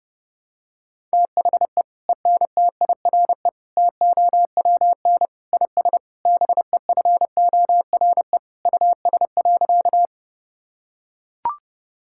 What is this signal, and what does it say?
Morse code, heard as a keyed tone, about 30 wpm, 700 Hz. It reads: THE ENTIRE TOWN IS BEFORE US.
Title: Sentences from Top 1000 Words - Review Part 5 - 30wpm